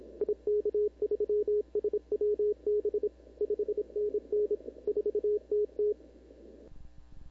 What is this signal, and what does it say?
Morse code, heard as a keyed tone, about 27 wpm, 420 Hz. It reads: IK3SWB 5NN 4TT
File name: CW / 2005-k5tr arrl cw dx -16:00